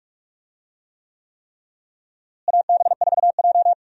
Morse code, heard as a keyed tone, about 44 wpm, 700 Hz. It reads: AB4J
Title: USA Call Signs - 4 Characters - Repeated 3x - 45wpm